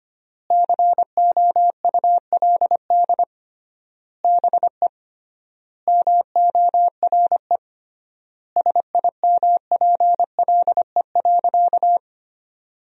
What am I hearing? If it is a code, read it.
Morse code, heard as a keyed tone, about 25 wpm, 700 Hz. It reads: COULD BE MORE SIMPLE.
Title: 3x Word-Spacing - Sentences from Top 500 Words - 25wpm